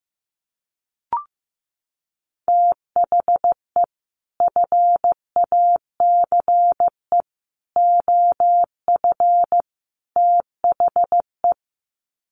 Morse code, heard as a keyed tone, about 15 wpm, 700 Hz. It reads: THE FACE OF THE
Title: Sets of 4 Words 15wpm